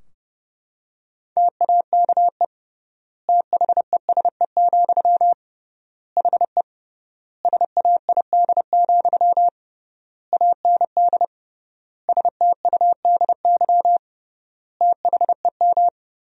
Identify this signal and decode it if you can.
Morse code, heard as a keyed tone, about 30 wpm, 700 Hz. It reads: TAKE THESE, HE SAID, AND STUDY THEM
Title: 3x Word-Spacing - Sentences from Top 300 Words - 30wpm